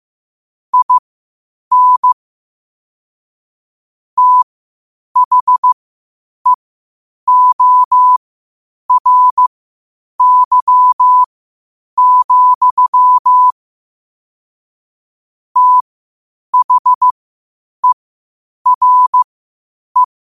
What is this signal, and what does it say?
Morse code, heard as a keyed tone, about 15 wpm, spaced as for 8 wpm, 1000 Hz. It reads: IN THEORY, THERE
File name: qotdfiles / 8wpm